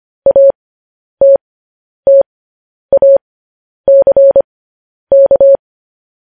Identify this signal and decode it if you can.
Morse code, heard as a keyed tone, about 25 wpm, spaced as for 10 wpm, 550 Hz. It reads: ATTACK
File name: podcast / News headlines-1x-25x10